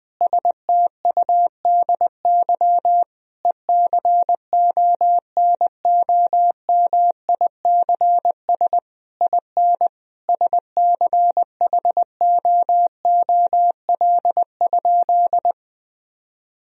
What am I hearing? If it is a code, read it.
Morse code, heard as a keyed tone, about 21 wpm, 700 Hz. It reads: STUDY ECONOMICS IN SCHOOL?
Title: Sentences from Top 1000 Words - Review Part 3 - 20wpm